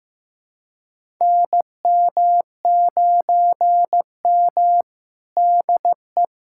Morse code, heard as a keyed tone, about 15 wpm, 700 Hz. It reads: NM9M DE
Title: USA Callsign DE Callsign - 4 Characters - 15wpm